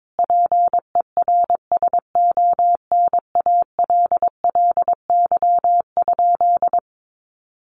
Morse code, heard as a keyed tone, about 22 wpm, 700 Hz. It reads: PERSONALLY?
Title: Sentences from Top 1400 Words 22wpm